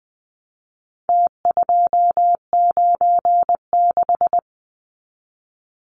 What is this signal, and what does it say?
Morse code, heard as a keyed tone, about 20 wpm, 700 Hz. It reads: T296
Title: Numbers - Contest Style - 4-Digit 20wpm